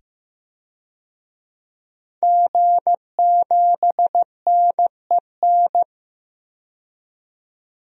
Morse code, heard as a keyed tone, about 15 wpm, 700 Hz. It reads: G7NEN
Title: DXCC-100 Call Signs - 5 Characters - 15wpm